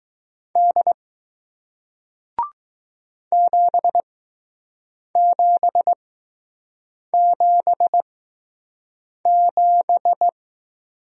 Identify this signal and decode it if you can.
Morse code, heard as a keyed tone, about 20 wpm, spaced as for 6 wpm, 700 Hz. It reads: D 7777